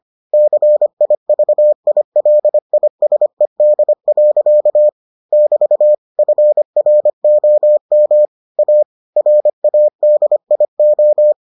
Morse code, heard as a keyed tone, about 25 wpm, 600 Hz. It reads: CIVILISED. <BT> FROM A RADIO